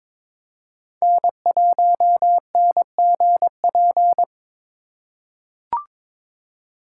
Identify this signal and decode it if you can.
Morse code, heard as a keyed tone, about 22 wpm, 700 Hz. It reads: N1NGP
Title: USA Call Signs - 5 Characters - 2x Character Spacing - 22wpm